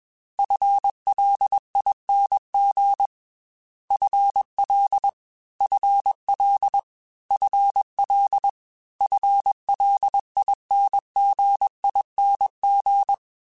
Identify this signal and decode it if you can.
Morse code, heard as a keyed tone, about 21 wpm, 800 Hz. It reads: FLING FL FL FL FLINGING
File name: FL / FL mp3s / FL Prefixes 3x2 @20